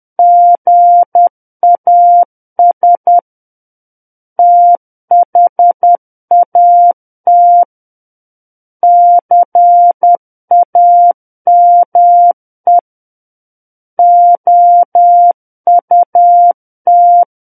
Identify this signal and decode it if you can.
Morse code, heard as a keyed tone, about 10 wpm, 700 Hz. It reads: GAS THAT CAME OUT
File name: Book One / 01-Chapter 1 - 10 WPM